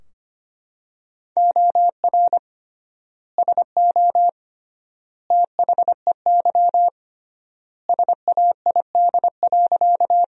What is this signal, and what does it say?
Morse code, heard as a keyed tone, about 25 wpm, 700 Hz. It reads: OR SO THEY SAID.
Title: 3x Word-Spacing - Sentences from Top 100 Words - 25wpm